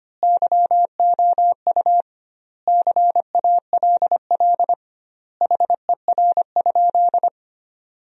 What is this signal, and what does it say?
Morse code, heard as a keyed tone, about 25 wpm, 700 Hz. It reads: YOU CALL HER?
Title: 2x Word-Spacing - Sentences from Top 100 Words - 25wpm